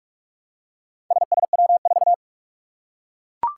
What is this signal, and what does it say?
Morse code, heard as a keyed tone, about 44 wpm, 700 Hz. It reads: SSW4
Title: Four Challenging Letter-Number Combinations 45wpm